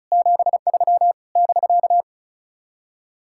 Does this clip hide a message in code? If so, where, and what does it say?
Yes: Morse code from 0.1 to 2.0 s, heard as a keyed tone, about 46 wpm, 700 Hz. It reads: 73 <BK>